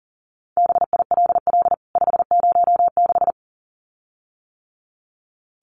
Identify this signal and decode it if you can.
Morse code, heard as a keyed tone, about 40 wpm, 700 Hz. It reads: BILL 506
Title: CWT Exchange 40wpm